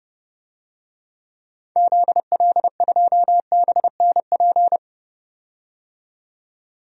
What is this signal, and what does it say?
Morse code, heard as a keyed tone, about 29 wpm, 700 Hz. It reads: ZL2BNP
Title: DXCC-100 Call Signs - 6 Characters - 30wpm